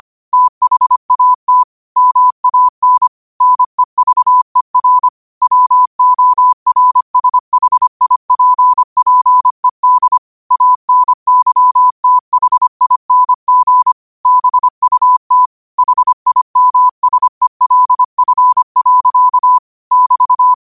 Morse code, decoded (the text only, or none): THAT MAN NEVER WORSHIPPED ANYTHING BUT HIMSELF. <BT>